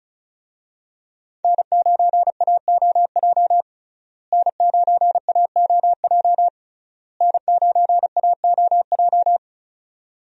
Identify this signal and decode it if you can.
Morse code, heard as a keyed tone, about 36 wpm, 700 Hz. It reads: N9AOJ N9AOJ N9AOJ
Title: USA Call Signs - 5 Characters - Repeated 3x - 35wpm